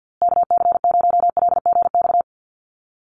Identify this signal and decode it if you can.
Morse code, heard as a keyed tone, about 50 wpm, 700 Hz. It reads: KC0LGX
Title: USA Call Signs Repeated 2x 50wpm